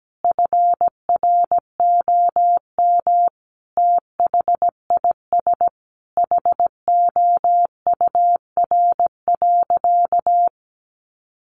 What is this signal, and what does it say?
Morse code, heard as a keyed tone, about 17 wpm, 700 Hz. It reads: FROM THIS HOUR.
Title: Sentences from Top 1100 Words 17wpm